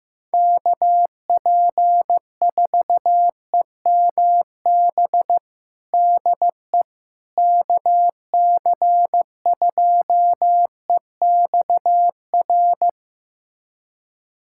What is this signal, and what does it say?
Morse code, heard as a keyed tone, about 15 wpm, 700 Hz. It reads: KP4EMB DE KC2EXR
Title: USA Callsign DE Callsign 15wpm